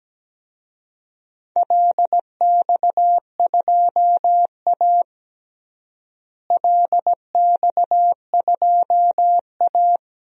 Morse code, heard as a keyed tone, about 17 wpm, 700 Hz. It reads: LX2A LX2A
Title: DXCC-100 Call Signs - 4 Characters - Repeated 2x - 17wpm